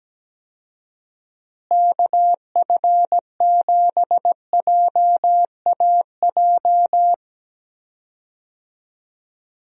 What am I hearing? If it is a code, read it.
Morse code, heard as a keyed tone, about 17 wpm, 700 Hz. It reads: KF7JAJ
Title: USA Call Signs - 6 Characters - Repeated 2x - 17wpm